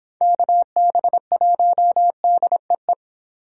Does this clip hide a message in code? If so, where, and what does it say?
Yes: Morse code from 0.2 to 2.9 s, heard as a keyed tone, about 26 wpm, 700 Hz. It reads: KB1DEE